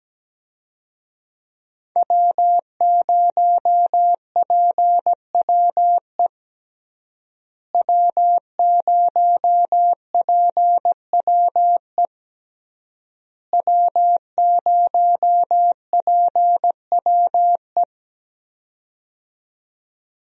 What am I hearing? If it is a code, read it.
Morse code, heard as a keyed tone, about 17 wpm, 700 Hz. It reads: W0PWE W0PWE W0PWE
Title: USA Call Signs - 5 Characters - Repeated 3x - 17wpm